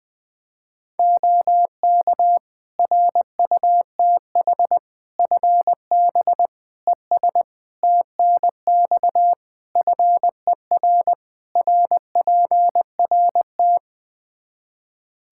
Morse code, heard as a keyed tone, about 20 wpm, 700 Hz. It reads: OK RUTH FB ES TNX FER RPRT
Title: Practice QSO - Exchange 2 - Info - 20wpm